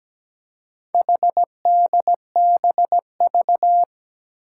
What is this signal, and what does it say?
Morse code, heard as a keyed tone, about 17 wpm, 700 Hz. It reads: HDBV
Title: Four Challenging Letter-Number Combinations 17wpm